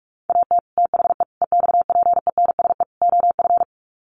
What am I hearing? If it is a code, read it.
Morse code, heard as a keyed tone, about 44 wpm, 700 Hz. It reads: AT THE EXPENSE OF
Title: Sets of 4 Words 45wpm